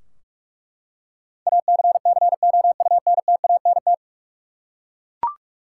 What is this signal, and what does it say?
Morse code, heard as a keyed tone, about 47 wpm, 700 Hz. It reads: ACCOUNTANT
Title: Practice QSO - Exchange 3 - More Info - 45wpm